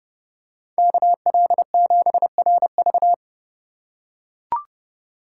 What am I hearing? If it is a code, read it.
Morse code, heard as a keyed tone, about 32 wpm, 700 Hz. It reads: KL7RV
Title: USA Call Signs - 4x Character Spacing - 30wpm